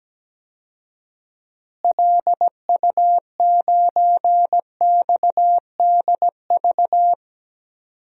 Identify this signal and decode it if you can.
Morse code, heard as a keyed tone, about 17 wpm, 700 Hz. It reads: LU9XDV
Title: DXCC-100 Call Signs - 6 Characters - Repeated 3x - 17wpm